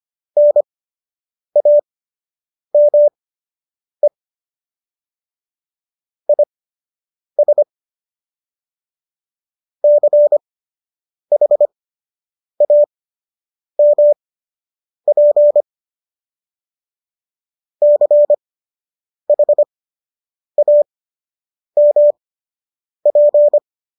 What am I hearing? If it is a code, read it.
Morse code, heard as a keyed tone, about 25 wpm, spaced as for 8 wpm, 590 Hz. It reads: NAME IS CHAMP CHAMP